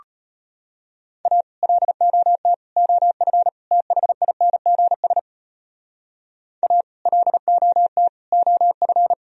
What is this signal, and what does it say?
Morse code, heard as a keyed tone, about 31 wpm, 700 Hz. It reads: A LOT OF THINGS A LOT OF